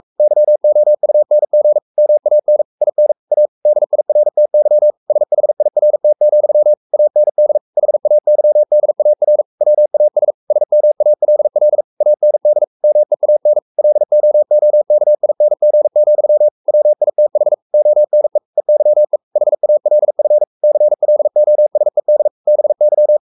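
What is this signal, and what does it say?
Morse code, heard as a keyed tone, about 44 wpm, 600 Hz. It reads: YOUNG MAN IN A DIRTY SHIRT, AND HAYDAR WAS SMALL AND MEAN LOOKING, WITH ONE EYE HALF CLOSED BY
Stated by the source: Morse code at 43 wpm sending Ken Follett's novel Never in English, from CW med Gnister.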